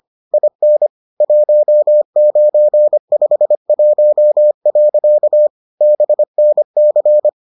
Morse code, heard as a keyed tone, about 25 wpm, 600 Hz. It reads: IN 1951. BNC